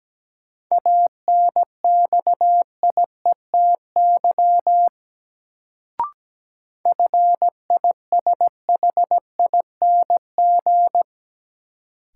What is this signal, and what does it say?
Morse code, heard as a keyed tone, about 17 wpm, 700 Hz. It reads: ANXIETY FISHING
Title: Sentences from Top 1600 Words - New Word Drill Alt - 17wpm